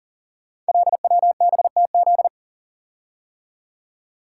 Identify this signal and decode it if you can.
Morse code, heard as a keyed tone, about 38 wpm, 690 Hz. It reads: LWBTZ
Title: Code Groups - 5 Letter:Numbers - 40wpm